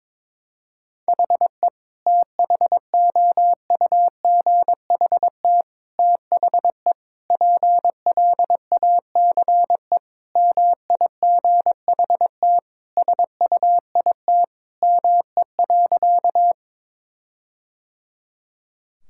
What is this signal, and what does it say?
Morse code, heard as a keyed tone, about 21 wpm, 700 Hz. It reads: HE THOUGHT THE PLACE MIGHT SUIT ME.
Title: Sentences from Top 1600 Words 22wpm